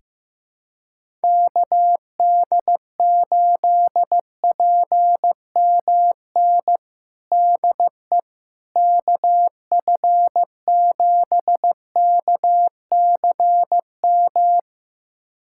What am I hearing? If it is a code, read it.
Morse code, heard as a keyed tone, about 15 wpm, 700 Hz. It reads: KD8PMN DE KF7KCM
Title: USA Callsign DE Callsign - 6 Characters - 15wpm